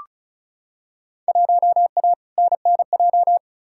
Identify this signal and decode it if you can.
Morse code, heard as a keyed tone, about 35 wpm, 700 Hz. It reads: 1A NNJ
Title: Field Day Exchanges Encore - 35wpm